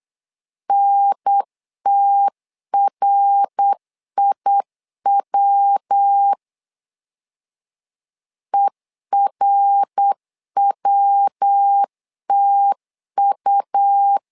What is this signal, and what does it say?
Morse code, heard as a keyed tone, about 8 wpm, 800 Hz. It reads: NTRIW ERWTU
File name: LEZIONI / 3 MWRO / 3 MWR 8wpm 2444